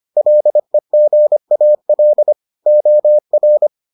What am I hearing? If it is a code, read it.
Morse code, heard as a keyed tone, about 25 wpm, 600 Hz. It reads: LEGAL OR